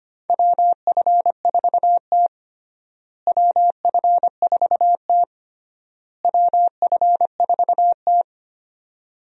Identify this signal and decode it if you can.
Morse code, heard as a keyed tone, about 25 wpm, 700 Hz. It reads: WF4T WF4T WF4T
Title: USA Call Signs - 4 Characters - Repeated 3x - 25wpm